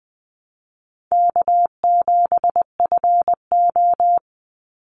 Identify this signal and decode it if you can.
Morse code, heard as a keyed tone, about 20 wpm, 700 Hz. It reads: K7FO
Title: NA SOTA Chaser Callsigns 20wpm